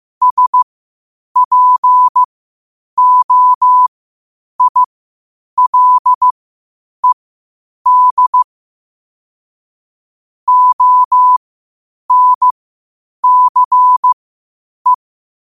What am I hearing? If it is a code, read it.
Morse code, heard as a keyed tone, about 15 wpm, spaced as for 8 wpm, 1000 Hz. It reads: SPOILED ONCE
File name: qotdfiles / 8wpm